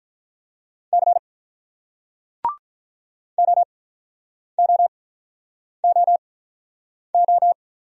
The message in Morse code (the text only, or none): C OOOO